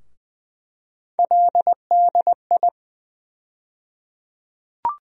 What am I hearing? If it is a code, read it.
Morse code, heard as a keyed tone, about 20 wpm, 700 Hz. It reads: LDI
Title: USA Call Sign Postfixes - 3 Letters 20wpm